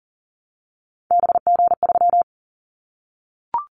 Morse code, heard as a keyed tone, about 40 wpm, 700 Hz. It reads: BG3
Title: Callsign Prefixes for Top 100 DXCC - 40wpm